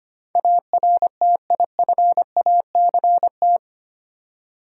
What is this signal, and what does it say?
Morse code, heard as a keyed tone, about 25 wpm, 690 Hz. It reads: ARTIFACT
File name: practice / 25wpm / Easy Words with Hints - 25wpm